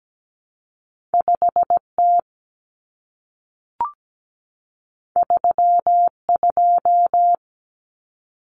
Morse code, heard as a keyed tone, about 17 wpm, 700 Hz. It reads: 5T 32
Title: Common Cut Numbers - 2-Digit 17wpm